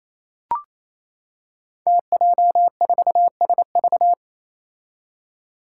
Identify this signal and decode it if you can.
Morse code, heard as a keyed tone, about 28 wpm, 700 Hz. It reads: TJ4SV
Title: Five Challenging Letter-Number Combinations 28wpm